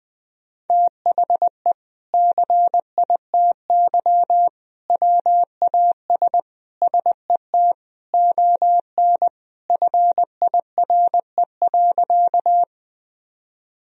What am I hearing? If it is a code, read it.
Morse code, heard as a keyed tone, about 20 wpm, 700 Hz. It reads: THE CITY WAS SET ON FIRE.
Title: Sentences from Top 1000 Words - Review Part 1 - 20wpm